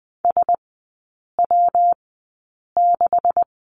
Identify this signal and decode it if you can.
Morse code, heard as a keyed tone, about 20 wpm, spaced as for 8 wpm, 700 Hz. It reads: SW6
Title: 127 Lesson 21e - Number 6 - Three Characters 20wpm